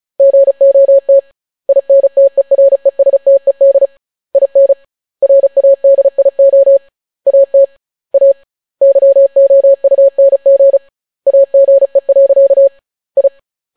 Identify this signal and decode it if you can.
Morse code, heard as a keyed tone, about 36 wpm, 550 Hz. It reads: GOT INTERESTED IN RADIO AT A YOUNG AGE. I